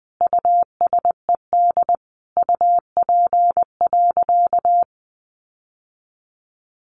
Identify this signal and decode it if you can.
Morse code, heard as a keyed tone, about 20 wpm, 700 Hz. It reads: USED UP.
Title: Sentences from Top 900 Words 20wpm